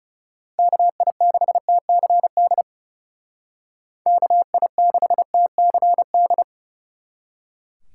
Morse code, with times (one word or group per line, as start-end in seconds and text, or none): 0.6-2.6 KI6TCD
4.1-6.4 KI6TCD